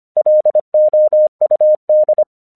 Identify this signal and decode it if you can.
Morse code, heard as a keyed tone, about 25 wpm, 610 Hz. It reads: LOUD